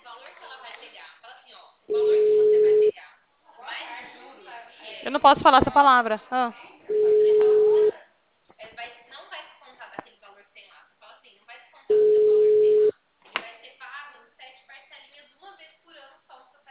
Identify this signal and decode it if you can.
Ringback tone (Europe, 425 Hz), with speech.